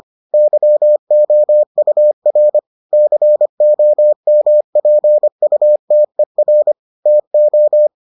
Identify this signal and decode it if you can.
Morse code, heard as a keyed tone, about 25 wpm, 600 Hz. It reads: YOUR COMPUTER TO